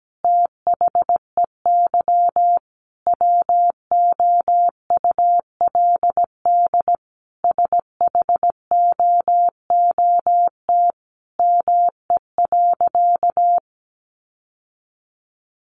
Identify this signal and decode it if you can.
Morse code, heard as a keyed tone, about 17 wpm, 700 Hz. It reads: THEY WOULD SHOOT ME.
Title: Sentences from Top 1500 Words 17wpm